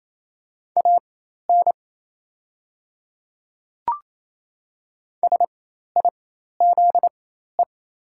Morse code, heard as a keyed tone, about 27 wpm, spaced as for 13 wpm, 690 Hz. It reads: AN SIZE
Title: 4x Character Spacing - Top 500 Words - 28wpm